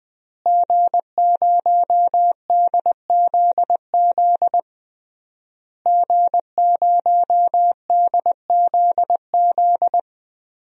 Morse code, heard as a keyed tone, about 20 wpm, 700 Hz. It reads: G0DZZ G0DZZ
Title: DXCC-100 Call Signs - 5 Characters - Repeated 3x - 20wpm